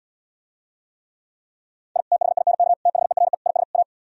Morse code, heard as a keyed tone, about 77 wpm, 690 Hz. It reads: I THINK THERE IS A